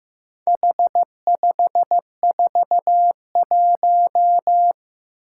Morse code, heard as a keyed tone, about 15 wpm, 700 Hz. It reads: H541
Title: Four Challenging Letter-Number Combinations 15wpm